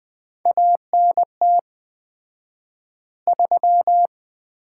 Morse code, heard as a keyed tone, about 20 wpm, 700 Hz. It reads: ANT 3
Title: Common QSO Phrases - 4x Word-Spacing - 20wpm